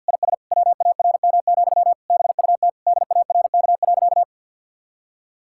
Morse code, heard as a keyed tone, about 50 wpm, 700 Hz. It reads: IS WARM, BUT DARK.
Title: Sentences from Top 1600 Words 50wpm